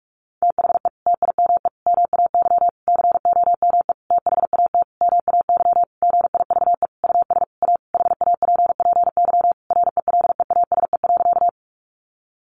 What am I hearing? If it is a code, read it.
Morse code, heard as a keyed tone, about 44 wpm, 700 Hz. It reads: THE TIME MAY COME THAT MAY GIVE US A HAPPY RELEASE.
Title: Sentences from Top 1100 Words 45wpm